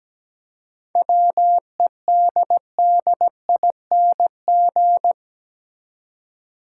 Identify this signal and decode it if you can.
Morse code, heard as a keyed tone, about 17 wpm, 700 Hz. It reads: WEDDING
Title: Seven Letter Words - 17wpm